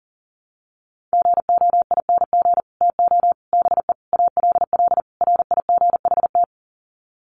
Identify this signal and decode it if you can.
Morse code, heard as a keyed tone, about 40 wpm, 700 Hz. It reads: GOING TO BE ALL RIGHT
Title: Sets of 5 Words Encore2 40wpm